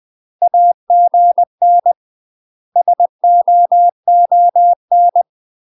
Morse code, heard as a keyed tone, about 20 wpm, 700 Hz. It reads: AGN SOON
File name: UK MASTER QSOs TRAINING / 20WPM / UK MASTER QSOs 0004